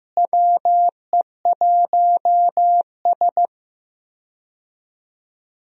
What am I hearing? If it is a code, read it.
Morse code, heard as a keyed tone, about 15 wpm, 700 Hz. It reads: WE1S
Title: USA Call Signs - 4 Characters - Repeated 2x - 15wpm